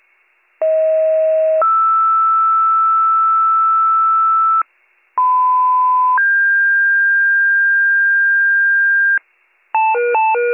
Two-tone page, heard as a one-second tone, then a three-second tone, about 640 then 1320 Hz, about 1010 then 1600 Hz, with noise. It is heard from 0.6 to 9.2 s.